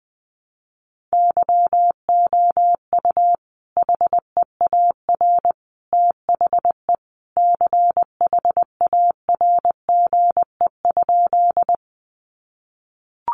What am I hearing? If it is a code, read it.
Morse code, heard as a keyed tone, about 20 wpm, 700 Hz. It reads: YOU HEAR THE CHARGE?
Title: Sentences from Top 900 Words 20wpm